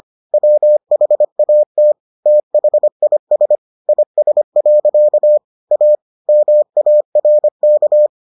Morse code, heard as a keyed tone, about 25 wpm, 600 Hz. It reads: WHAT THIS IS. A MARK